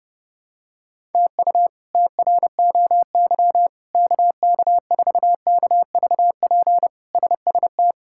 Morse code, heard as a keyed tone, about 32 wpm, 700 Hz. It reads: TU TROY KK4KVP SST
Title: K1USN SST QSO Practice - 30wpm